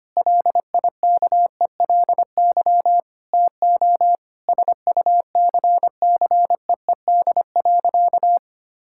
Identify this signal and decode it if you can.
Morse code, heard as a keyed tone, about 25 wpm, 700 Hz. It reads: LIKELY TO SUCCEED.
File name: practice / 25wpm / Sentences with 4-Word Phrases - 25wpm